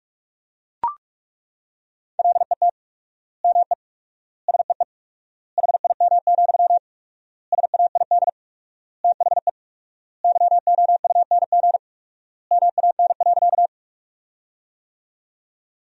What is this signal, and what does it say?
Morse code, heard as a keyed tone, about 41 wpm, 700 Hz. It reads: LET ME SEE HIM, SAID THE YOUNG MAN.
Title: 4x Word-Spacing - Sentences from Top 400 Words - 45wpm